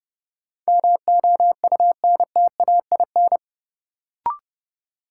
Morse code, heard as a keyed tone, about 30 wpm, 700 Hz. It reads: MOUNTAIN
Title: Sentences from Top 1000 Words - Review Part 1 - New Word Drill - 30wpm